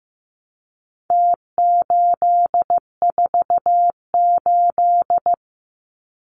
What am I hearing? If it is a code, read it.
Morse code, heard as a keyed tone, about 15 wpm, 700 Hz. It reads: T848
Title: Numbers - Contest Style - 4-Digit 15wpm